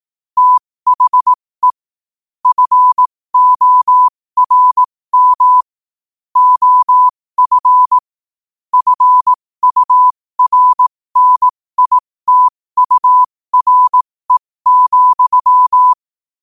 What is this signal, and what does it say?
Morse code, heard as a keyed tone, about 18 wpm, spaced as for 15 wpm, 1000 Hz. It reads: THE FORM OF FURNITURE,
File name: qotdfiles / 15wpm